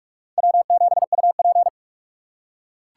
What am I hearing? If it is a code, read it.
Morse code, heard as a keyed tone, about 40 wpm, 700 Hz. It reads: W7UP